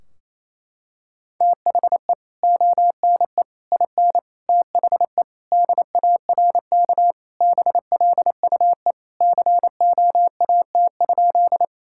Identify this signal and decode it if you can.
Morse code, heard as a keyed tone, about 28 wpm, 700 Hz. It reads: THE ONE IN THE DARK BLUE COAT?
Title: Sentences from Top 1600 Words 28wpm